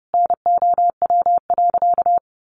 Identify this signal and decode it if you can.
Morse code, heard as a keyed tone, about 30 wpm, 700 Hz. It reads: NOW.
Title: Sentences from Top 1100 Words 30wpm